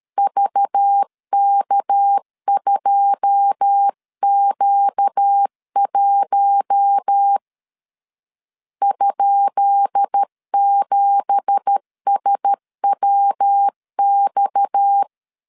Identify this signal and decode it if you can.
Morse code, heard as a keyed tone, about 13 wpm, 800 Hz. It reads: VK2Q1 ?7SWX
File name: LEZIONI / 10 SIGNS / 10D COMA 12wpm 2544